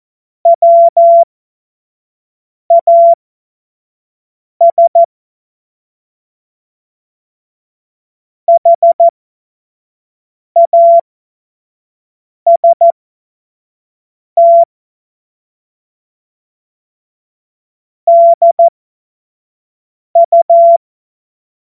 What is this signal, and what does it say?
Morse code, heard as a keyed tone, about 14 wpm, spaced as for 5 wpm, 680 Hz. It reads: WAS HAST DU
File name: Morsekurs / Witze / Witz 15. August 2023 25 F70